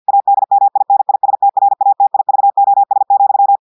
Morse code, heard as a keyed tone, about 50 wpm, 800 Hz. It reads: ADMINISTRATIVOS,
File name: audios / portugais / portugais texte en clair 2 vitesse 50 mots minutes